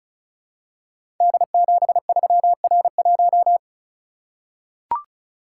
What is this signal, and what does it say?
Morse code, heard as a keyed tone, about 36 wpm, 700 Hz. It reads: D73R1
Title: Code Groups - 5 Letter:Numbers - 35wpm